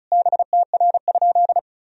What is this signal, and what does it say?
Morse code, heard as a keyed tone, about 36 wpm, 700 Hz. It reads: BTR?
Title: Sentences from Top 400 Words and Abbreviations 35wpm